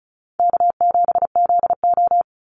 Morse code, heard as a keyed tone, about 34 wpm, 700 Hz. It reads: K7ZO
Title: NA SOTA Chaser Callsigns 35wpm